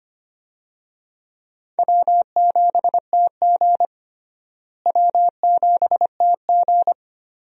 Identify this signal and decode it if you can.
Morse code, heard as a keyed tone, about 25 wpm, 700 Hz. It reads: W7TG W7TG
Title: USA Call Signs - 4 Characters - Repeated 2x - 25wpm